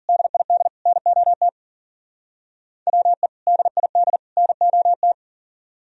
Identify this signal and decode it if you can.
Morse code, heard as a keyed tone, about 53 wpm, 700 Hz. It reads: DID NOT WE DID NOT